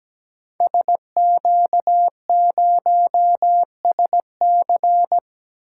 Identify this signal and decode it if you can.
Morse code, heard as a keyed tone, about 17 wpm, 700 Hz. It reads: SQ0SC
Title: Code Groups - 5 Letter:Numbers - 17wpm